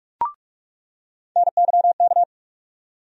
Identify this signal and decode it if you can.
Morse code, heard as a keyed tone, about 43 wpm, 700 Hz. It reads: NYK
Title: USA Call Sign Postfixes - 3 Letters 45wpm